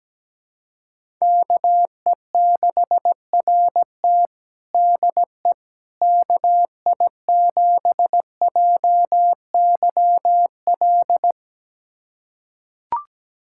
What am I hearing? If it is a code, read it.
Morse code, heard as a keyed tone, about 17 wpm, 700 Hz. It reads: KE6RT DE KI7JYL
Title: USA Callsign DE Callsign 17wpm